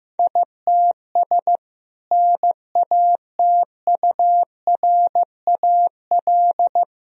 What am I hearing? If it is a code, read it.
Morse code, heard as a keyed tone, about 15 wpm, 700 Hz. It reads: ITS NATURAL
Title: Sentences from Top 1000 Words - Review Part 4 - 15wpm